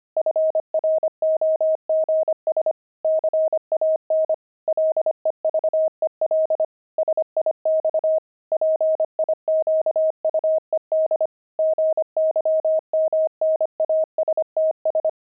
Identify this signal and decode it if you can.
Morse code, heard as a keyed tone, about 25 wpm, 630 Hz. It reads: FROGS CAN LEVEL SIX PIQUED GYMNASTS
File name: pangram-fs-25-cs-25-frogs